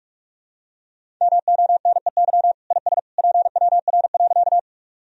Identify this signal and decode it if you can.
Morse code, heard as a keyed tone, about 46 wpm, 690 Hz. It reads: MONEY IS PWR.
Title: Sentences from Top 500 Words and Abbreviations 45wpm